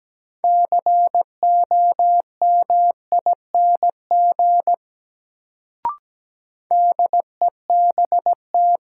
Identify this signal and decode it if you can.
Morse code, heard as a keyed tone, about 17 wpm, 700 Hz. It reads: COMING DEBT
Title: Sentences from Top 1300 Words - New Word Drill Alt - 17wpm